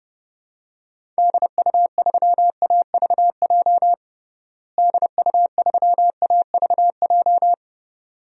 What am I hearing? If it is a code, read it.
Morse code, heard as a keyed tone, about 30 wpm, 700 Hz. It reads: DU3AVJ DU3AVJ
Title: DXCC-100 Call Signs - 6 Characters - Repeated 2x - 30wpm